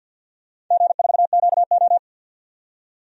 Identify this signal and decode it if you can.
Morse code, heard as a keyed tone, about 47 wpm, 700 Hz. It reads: G4QO
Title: DXCC-100 Call Signs - 4 Characters - 50wpm